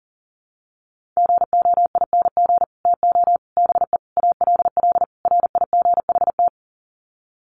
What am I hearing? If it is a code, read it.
Morse code, heard as a keyed tone, about 40 wpm, 700 Hz. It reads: GOING TO BE ALL RIGHT
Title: Sets of 5 Words Encore2 40wpm